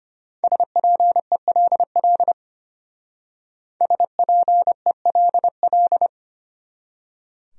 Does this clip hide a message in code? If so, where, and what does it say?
Yes: Morse code from 0.4 to 6.1 s, heard as a keyed tone, about 32 wpm, 700 Hz. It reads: SPELL SPELL